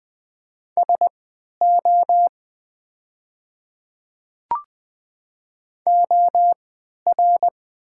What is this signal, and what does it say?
Morse code, heard as a keyed tone, about 20 wpm, spaced as for 11 wpm, 700 Hz. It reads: SO OR
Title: 3x Character Spacing - Top 100 Words - 20wpm